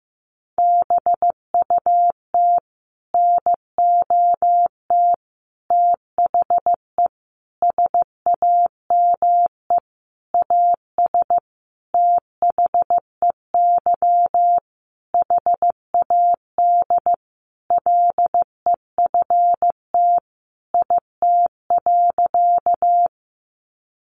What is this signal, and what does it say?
Morse code, heard as a keyed tone, about 15 wpm, 700 Hz. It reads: BUT NOT THE SAME AS THEY HAD LEFT IT.
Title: A Sound Of Thunder 15wpm